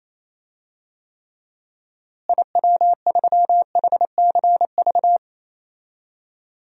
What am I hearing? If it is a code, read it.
Morse code, heard as a keyed tone, about 28 wpm, 700 Hz. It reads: IW3HCV
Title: DXCC-100 Call Signs - 6 Characters - Repeated 3x - 28wpm